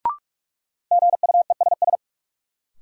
Morse code, heard as a keyed tone, about 46 wpm, 700 Hz. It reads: GUESS
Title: Sentences from Top 1500 Words - New Word Drill - 45wpm